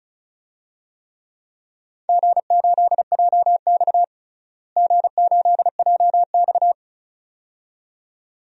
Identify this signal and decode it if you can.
Morse code, heard as a keyed tone, about 36 wpm, 700 Hz. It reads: G8JX G8JX
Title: DXCC-100 Call Signs - 4 Characters - Repeated 2x - 35wpm